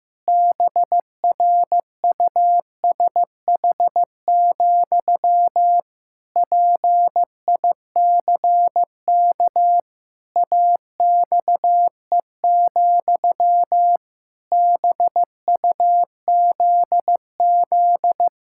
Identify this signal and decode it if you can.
Morse code, heard as a keyed tone, about 15 wpm, 700 Hz. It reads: BRUSH, PICK AXE, BUZZ
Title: Pangram Challenge v2 - 15wpm